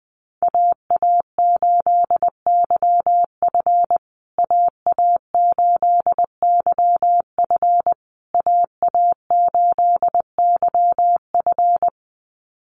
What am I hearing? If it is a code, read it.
Morse code, heard as a keyed tone, about 20 wpm, 700 Hz. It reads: AA8YF AA8YF AA8YF
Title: USA Call Signs Repeated 3x 20wpm